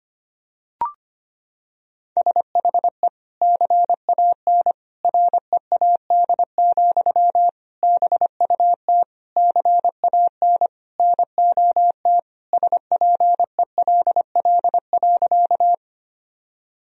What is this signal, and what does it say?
Morse code, heard as a keyed tone, about 25 wpm, 700 Hz. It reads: SHE CAN READ, BUT CAN NOT SPELL.
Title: Sentences from Top 1000 Words - Review Part 1 - 25wpm